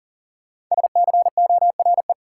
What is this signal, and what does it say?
Morse code, heard as a keyed tone, about 40 wpm, 700 Hz. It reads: SCORE
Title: Sentences from Top 1600 Words - New Word Drill - 40wpm